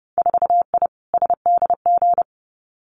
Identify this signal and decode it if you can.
Morse code, heard as a keyed tone, about 30 wpm, 700 Hz. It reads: 4I SDG
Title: Winter Field Day Exchanges Encore 30wpm